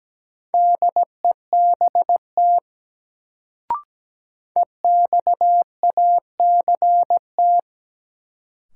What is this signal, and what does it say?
Morse code, heard as a keyed tone, about 17 wpm, 700 Hz. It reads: DEBT EXACT
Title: Sentences from Top 1300 Words - New Word Drill - 17wpm